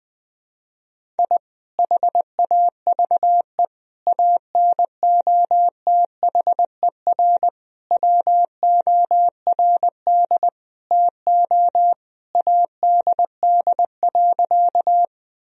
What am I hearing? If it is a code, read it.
Morse code, heard as a keyed tone, about 20 wpm, 700 Hz. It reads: I HAVE ANOTHER WORD TO ADD.
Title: Sentences from Top 1000 Words - Review Part 1 - 20wpm